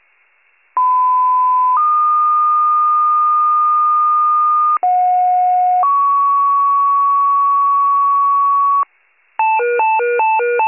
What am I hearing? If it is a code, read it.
Two-tone page, heard as a one-second tone, then a three-second tone, about 1010 then 1220 Hz, about 730 then 1120 Hz. It is heard from 0.8 to 8.8 s.